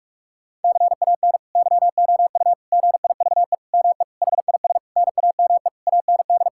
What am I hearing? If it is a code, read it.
Morse code, heard as a keyed tone, about 42 wpm, 700 Hz. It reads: CAN YOU GIVE ME HIS NAME AND
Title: Sentences from Top 1000 Words - Review Part 6 - 45wpm